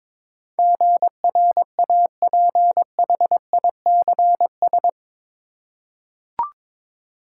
Morse code, heard as a keyed tone, about 22 wpm, 700 Hz. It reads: GRAPHICS
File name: practice / 22wpm / Eight Letter Words - 22wpm